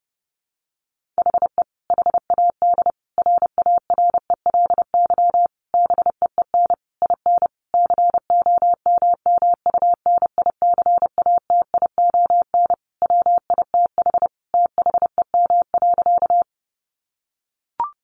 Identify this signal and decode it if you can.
Morse code, heard as a keyed tone, about 28 wpm, 700 Hz. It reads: HE HAD RARELY BEEN IN COMMUNICATION WITH THEM.
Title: Sentences from Top 1400 Words 30wpm